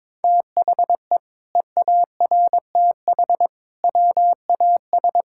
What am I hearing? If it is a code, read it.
Morse code, heard as a keyed tone, about 22 wpm, 700 Hz. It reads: THE EARTH WAS
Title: Sentences from Top 1000 Words - Review Part 6 - 22wpm